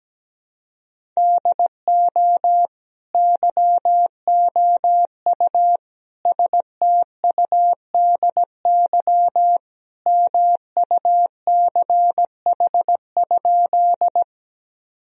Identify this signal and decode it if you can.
Morse code, heard as a keyed tone, about 17 wpm, 700 Hz. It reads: DO YOU STUDY MUCH?
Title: Sentences from Top 1000 Words - Review Part 1 - 17wpm